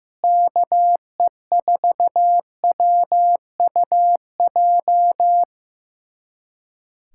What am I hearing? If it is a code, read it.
Morse code, heard as a keyed tone, about 15 wpm, 700 Hz. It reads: KE4WUJ
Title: USA Callsign DE Callsign 15wpm